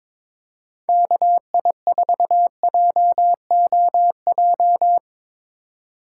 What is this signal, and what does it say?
Morse code, heard as a keyed tone, about 23 wpm, 700 Hz. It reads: KI4JOJ
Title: USA Call Signs - 6 Characters - 2x Character Spacing - 22wpm